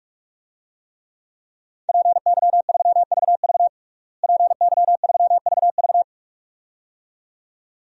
Morse code, heard as a keyed tone, about 46 wpm, 700 Hz. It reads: PY3VV PY3VV
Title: DXCC-100 Call Signs - 5 Characters - Repeated 2x - 45wpm